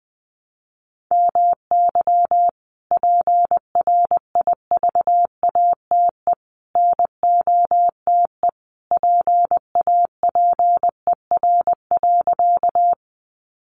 Morse code, heard as a keyed tone, about 20 wpm, 700 Hz. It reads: MY PRIVATE NOTE PAPER.
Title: Sentences from Top 1000 Words 20wpm